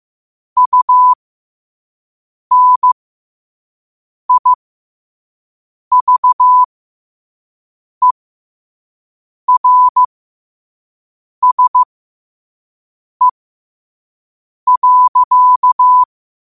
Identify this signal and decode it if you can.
Morse code, heard as a keyed tone, about 15 wpm, spaced as for 5 wpm, 1000 Hz. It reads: UNIVERSE.
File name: qotdfiles / 5wpm